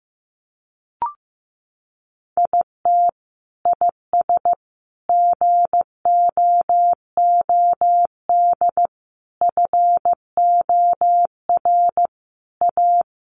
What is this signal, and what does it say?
Morse code, heard as a keyed tone, about 15 wpm, 700 Hz. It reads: IT IS GOOD FOR A
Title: Sentences from Top 1400 Words 15wpm